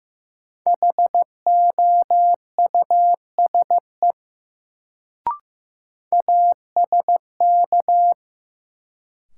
Morse code, heard as a keyed tone, about 15 wpm, 700 Hz. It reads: HOUSE ASK
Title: Sentences from Top 200 Words - New Word Drill - 15wpm